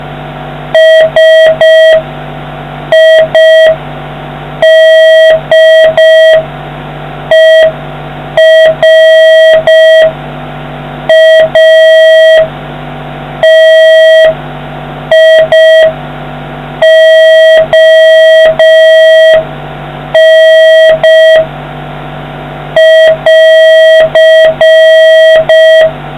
Morse code, heard as a keyed tone, about 5 wpm, spaced as for 4 wpm, 1950 Hz, with noise. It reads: SIDERATION<AR>